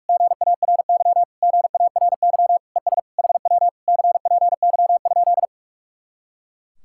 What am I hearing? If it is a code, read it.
Morse code, heard as a keyed tone, about 58 wpm, 700 Hz. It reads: GARY GARY ES HW CPY?